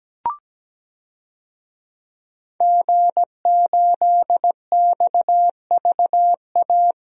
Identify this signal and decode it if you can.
Morse code, heard as a keyed tone, about 17 wpm, 700 Hz. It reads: G8XVA
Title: DXCC-100 Call Signs - 5 Characters - Repeated 2x - 17wpm